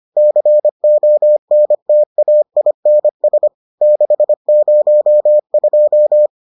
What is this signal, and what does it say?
Morse code, heard as a keyed tone, about 25 wpm, 600 Hz. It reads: CONTAINS 602